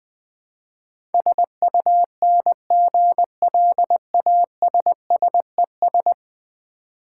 Morse code, heard as a keyed tone, about 20 wpm, 700 Hz. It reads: SUNGLASSES
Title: Easy Words with Hints - 20wpm